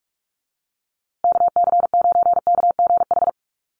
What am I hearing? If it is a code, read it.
Morse code, heard as a keyed tone, about 44 wpm, 700 Hz. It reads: KC9KGH
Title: USA Call Signs Repeated 3x 45wpm